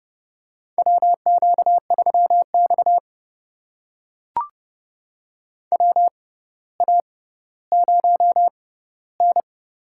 Morse code, heard as a keyed tone, about 32 wpm, spaced as for 16 wpm, 700 Hz. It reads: WQ3X WA0N
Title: USA Call Signs - 4 Characters - 6x Character Spacing - 30wpm